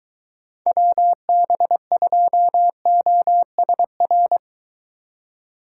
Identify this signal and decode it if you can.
Morse code, heard as a keyed tone, about 22 wpm, 700 Hz. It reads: WB2OSR